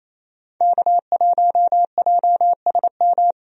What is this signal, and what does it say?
Morse code, heard as a keyed tone, about 28 wpm, 700 Hz. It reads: K1JSM
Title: USA Call Signs - 5 Characters - 28wpm